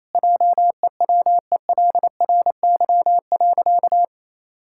Morse code, heard as a keyed tone, about 28 wpm, 700 Hz. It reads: JEWELRY.
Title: Pangram Challenge v2 - 28wpm